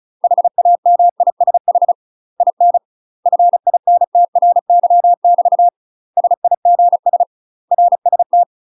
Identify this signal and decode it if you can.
Morse code, heard as a keyed tone, about 34 wpm, 700 Hz. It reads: HAMISH IN FINTRY <BT> SIGS RST
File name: UK MASTER QSOs TRAINING / 35WPM / UK MASTER QSOs 0015